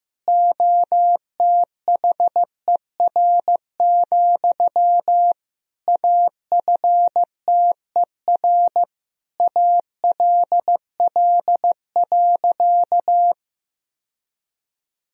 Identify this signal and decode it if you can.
Morse code, heard as a keyed tone, about 16 wpm, 700 Hz. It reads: OTHER, AFTER ALL.
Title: Sentences from Top 1000 Words - Review Part 1 - 15wpm